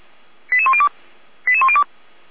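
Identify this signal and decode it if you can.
Five-tone selective call (ZVEI), heard as quick runs of five tones, tones of about 70 ms. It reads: 88102 88102
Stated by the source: ZVEI five-tone selective call received on an SDR.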